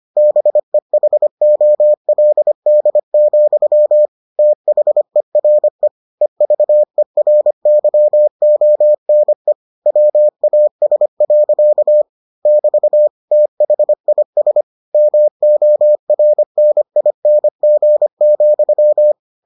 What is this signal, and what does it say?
Morse code, heard as a keyed tone, about 25 wpm, 600 Hz. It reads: BEHOLD, THERE EVERYONE WAS. <BT> THIS MORNING,